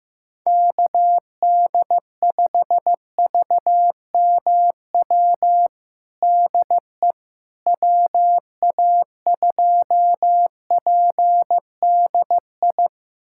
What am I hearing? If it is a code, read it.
Morse code, heard as a keyed tone, about 15 wpm, 700 Hz. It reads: KD5VMW DE WA2PDI
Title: USA Callsign DE Callsign - 6 Characters - 15wpm